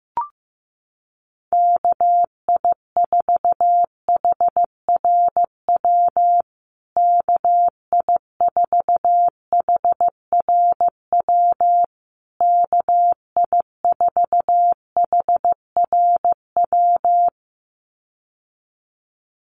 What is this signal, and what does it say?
Morse code, heard as a keyed tone, about 15 wpm, 700 Hz. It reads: KI4HRW KI4HRW KI4HRW
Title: USA Call Signs Repeated 3x 15wpm